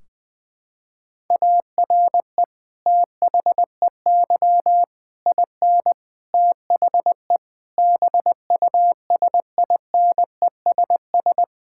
Morse code, heard as a keyed tone, about 20 wpm, 700 Hz. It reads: ARE THEY IN THE BUSINESS
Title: Sentences from Top 1600 Words 20wpm